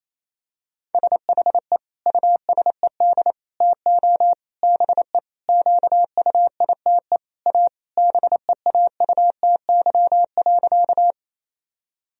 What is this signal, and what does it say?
Morse code, heard as a keyed tone, about 27 wpm, 700 Hz. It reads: SHE USED TO BE QUITE A BEAUTY.
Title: Sentences from Top 1000 Words - Review Part 2 - 28wpm